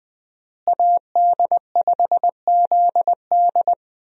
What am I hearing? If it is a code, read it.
Morse code, heard as a keyed tone, about 20 wpm, 700 Hz. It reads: AD5ZD
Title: USA Call Signs - 5 Characters - Repeated 3x - 20wpm